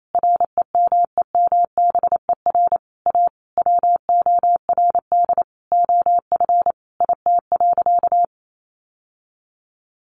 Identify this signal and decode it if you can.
Morse code, heard as a keyed tone, about 28 wpm, 700 Hz. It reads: REMEMBER A WORD OF IT.
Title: Sentences from Top 1200 Words 28wpm